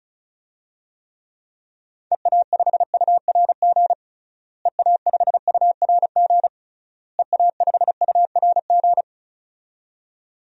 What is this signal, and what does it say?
Morse code, heard as a keyed tone, about 36 wpm, 700 Hz. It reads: EA5URG EA5URG EA5URG
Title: DXCC-100 Call Signs - 6 Characters - Repeated 3x - 35wpm